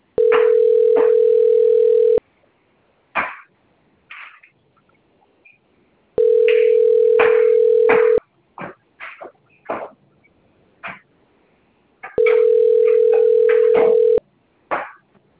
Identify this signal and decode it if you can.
Ringback tone (North America), 440 + 480 Hz.
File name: rg-1600-unknown- - - .4823